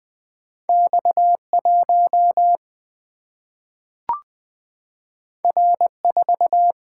Morse code, heard as a keyed tone, about 20 wpm, 700 Hz. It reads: X 1 R 4
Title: Single-Letter Single-Number 20wpm